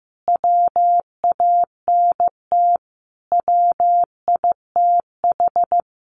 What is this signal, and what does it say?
Morse code, heard as a keyed tone, about 15 wpm, 700 Hz. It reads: WANT WITH
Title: Sentences from Top 1400 Words 15wpm